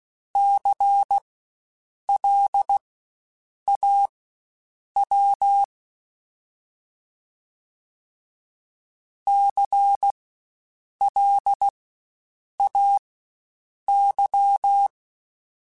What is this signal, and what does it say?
Morse code, heard as a keyed tone, about 16 wpm, spaced as for 7 wpm, 800 Hz. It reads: CLAW CLAY
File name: Spaced Alphas / Spaced Alphas mp3s / Cs sp @15